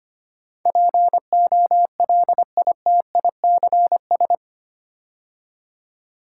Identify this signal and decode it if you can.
Morse code, heard as a keyed tone, about 25 wpm, 690 Hz. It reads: POLITICS
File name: 8 Letter Words - First 3 Letters Given - 25wpm